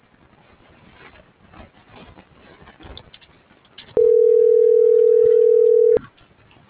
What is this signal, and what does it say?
Ringback tone (North America), 440 + 480 Hz.